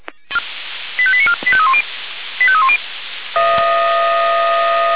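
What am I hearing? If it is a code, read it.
Five-tone selective call (ZVEI), heard as quick runs of five tones, tones of about 70 ms. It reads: 85894 85310 85310
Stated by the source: ZVEI five-tone call received by SDR, archived on the Feuerwehr Orlamünde audio server (f85360).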